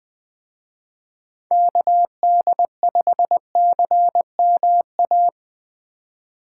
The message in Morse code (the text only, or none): KD5CMA